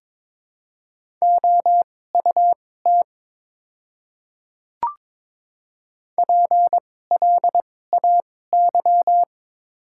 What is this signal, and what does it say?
Morse code, heard as a keyed tone, about 22 wpm, spaced as for 16 wpm, 700 Hz. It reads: OUT PLAY
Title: 2x Character Spacing - Top 200 Words - 22wpm